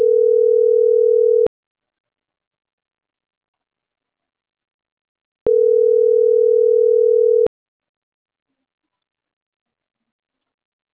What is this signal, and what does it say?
Ringback tone (North America), 440 + 480 Hz.